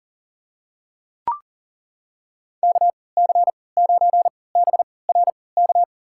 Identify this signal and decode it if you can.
Morse code, heard as a keyed tone, about 40 wpm, spaced as for 23 wpm, 700 Hz. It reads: KC9BRK
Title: USA Call Signs - 6 Characters - 3x Character Spacing - 40wpm